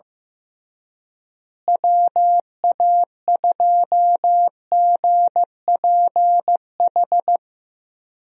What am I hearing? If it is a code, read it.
Morse code, heard as a keyed tone, about 15 wpm, 700 Hz. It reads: WA2GPH
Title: USA Call Signs - 6 Characters - Repeated 3x - 15wpm